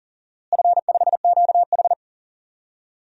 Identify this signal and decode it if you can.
Morse code, heard as a keyed tone, about 40 wpm, 700 Hz. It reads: F5QH
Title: DXCC-100 Call Signs - 4 Characters - Repeated 3x - 40wpm